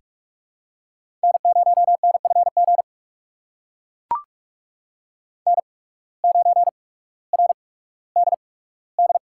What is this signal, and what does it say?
Morse code, heard as a keyed tone, about 41 wpm, spaced as for 19 wpm, 700 Hz. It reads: N0NFG N9RDD
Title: USA Call Signs - 5 Characters - 8x Character Spacing - 45wpm